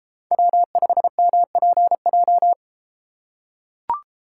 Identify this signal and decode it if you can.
Morse code, heard as a keyed tone, about 34 wpm, 700 Hz. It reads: W5MPJ